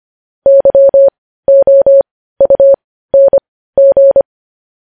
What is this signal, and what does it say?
Morse code, heard as a keyed tone, about 25 wpm, spaced as for 15 wpm, 550 Hz. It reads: YOUNG
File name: podcast / News headlines-1x-25x15